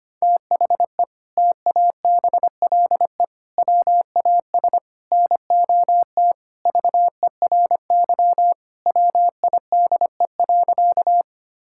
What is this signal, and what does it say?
Morse code, heard as a keyed tone, about 25 wpm, 700 Hz. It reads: THE TABLE WAS NOT VERY WIDE.
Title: Sentences from Top 1000 Words - Review Part 6 - 25wpm